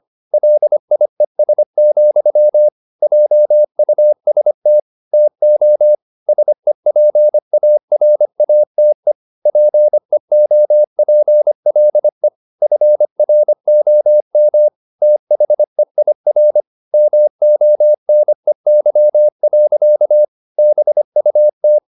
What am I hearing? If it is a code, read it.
Morse code, heard as a keyed tone, about 25 wpm, 600 Hz. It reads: LIES, JUST TO SEPARATE PEOPLE FROM THEIR MONEY. BUT